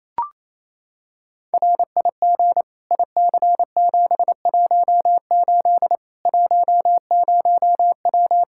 Morse code, heard as a keyed tone, about 28 wpm, 700 Hz. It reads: RIG IC718 10W
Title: Practice QSO - Exchange 2 - Info - 28wpm